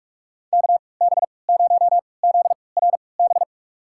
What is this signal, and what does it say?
Morse code, heard as a keyed tone, about 46 wpm, spaced as for 25 wpm, 700 Hz. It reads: KB0ZRB
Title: USA Call Signs - 6 Characters - 3x Character Spacing - 45wpm